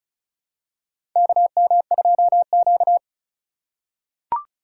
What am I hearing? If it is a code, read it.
Morse code, heard as a keyed tone, about 36 wpm, 700 Hz. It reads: KM2Q
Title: USA Call Signs - 4 Characters - 4x Character Spacing - 35wpm